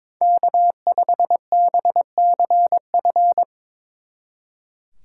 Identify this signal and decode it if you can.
Morse code, heard as a keyed tone, about 25 wpm, 700 Hz. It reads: K5BCF
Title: Speed Racing - USA Call Signs 22wpm